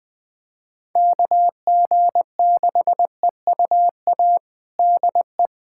Morse code, heard as a keyed tone, about 20 wpm, 700 Hz. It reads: KG6EUA DE
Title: USA Callsign DE Callsign - 6 Characters - 20wpm